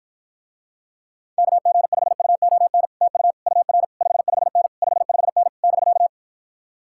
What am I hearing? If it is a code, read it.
Morse code, heard as a keyed tone, about 53 wpm, 700 Hz. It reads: KG5UON TU UR 55N 55N <BK>